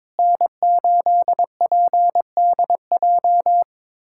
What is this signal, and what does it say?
Morse code, heard as a keyed tone, about 22 wpm, 700 Hz. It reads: N8PDJ
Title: USA Callsign DE Callsign - 5 Characters - 22wpm